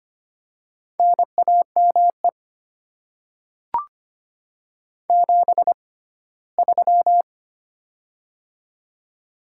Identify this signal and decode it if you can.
Morse code, heard as a keyed tone, about 25 wpm, spaced as for 17 wpm, 700 Hz. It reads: NAME 73
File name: Top 20 QSO Elements - 6x Character Spacing - 25wpm